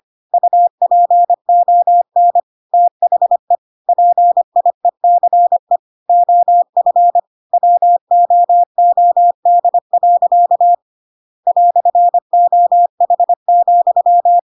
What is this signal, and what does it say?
Morse code, heard as a keyed tone, about 25 wpm, 700 Hz. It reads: UPON THE PIECE OF WOOD. "OH,
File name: Pinocchio / Pinocchio ch1 25 25 ik1hge